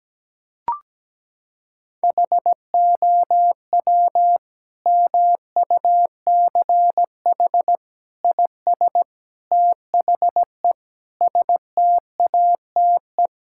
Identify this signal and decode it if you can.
Morse code, heard as a keyed tone, about 17 wpm, 700 Hz. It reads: HOW MUCH IS THE STATE
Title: Sentences from Top 1000 Words - Review Part 5 - 17wpm